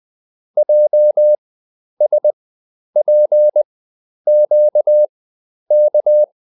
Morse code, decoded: JSPQK